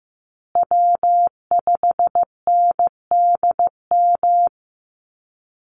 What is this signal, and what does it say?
Morse code, heard as a keyed tone, about 15 wpm, 700 Hz. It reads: W5NDM
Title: USA Call Signs Repeated 2x 15wpm